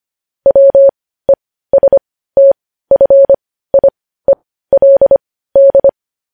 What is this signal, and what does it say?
Morse code, heard as a keyed tone, about 25 wpm, spaced as for 15 wpm, 550 Hz. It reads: WESTFIELD